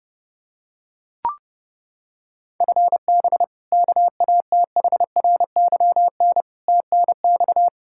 Morse code, heard as a keyed tone, about 32 wpm, 700 Hz. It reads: FB KATHRYN TNX
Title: Practice QSO - Exchange 3 - More Info - 30wpm